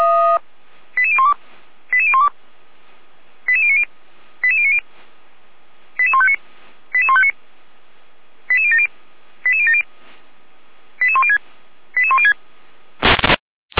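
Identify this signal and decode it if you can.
Five-tone selective call (ZVEI), heard as quick runs of five tones, tones of about 70 ms. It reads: 88012 88012 88090 88090 88159 88159 88070 88070 88106 88106
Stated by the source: ZVEI five-tone call received on an SDR.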